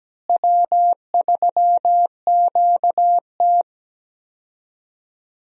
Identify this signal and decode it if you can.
Morse code, heard as a keyed tone, about 17 wpm, 700 Hz. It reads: W3QT
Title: USA Call Signs - 4 Characters - Repeated 2x - 17wpm